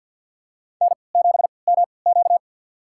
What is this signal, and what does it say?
Morse code, heard as a keyed tone, about 49 wpm, spaced as for 28 wpm, 700 Hz. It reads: N7MQ
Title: USA Call Signs - 4 Characters - 3x Character Spacing - 50wpm